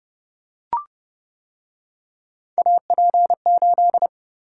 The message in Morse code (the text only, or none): AP8